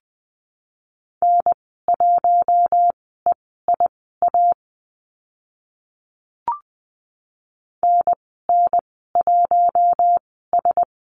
Morse code, heard as a keyed tone, about 20 wpm, spaced as for 14 wpm, 700 Hz. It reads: N1EIA NN1S
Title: 042 Lesson 8g - Number 1 - Call Signs 20wpm